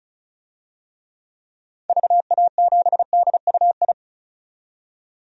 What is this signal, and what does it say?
Morse code, heard as a keyed tone, about 36 wpm, 700 Hz. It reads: VA7DUI
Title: DXCC-100 Call Signs - 6 Characters - 35wpm